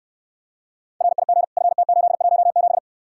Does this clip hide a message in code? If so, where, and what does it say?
Yes: Morse code from 1.0 to 2.8 s, heard as a keyed tone, about 69 wpm, 700 Hz. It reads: RIG FT817